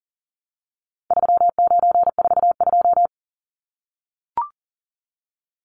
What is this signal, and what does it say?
Morse code, heard as a keyed tone, about 40 wpm, 700 Hz. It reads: 3942
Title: Numbers - 4-Digit 40wpm